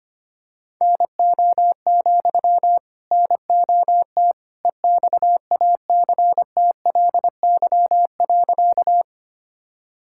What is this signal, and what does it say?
Morse code, heard as a keyed tone, about 25 wpm, 700 Hz. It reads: NO, NOT EXACTLY.
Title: Sentences from Top 1000 Words - Review Part 6 - 25wpm